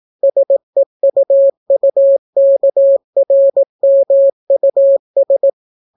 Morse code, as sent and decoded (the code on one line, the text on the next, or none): ... ..- ..- -.- .-. -- ..- ...
SEUUKRMUS